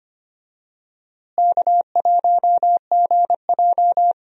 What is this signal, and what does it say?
Morse code, heard as a keyed tone, about 25 wpm, 700 Hz. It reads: K1GJ
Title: USA Call Signs - 4 Characters - Repeated 2x - 25wpm